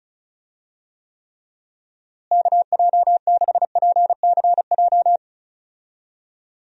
Morse code, heard as a keyed tone, about 36 wpm, 700 Hz. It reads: KJ6PCJ
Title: USA Call Signs - 6 Characters - Repeated 2x - 35wpm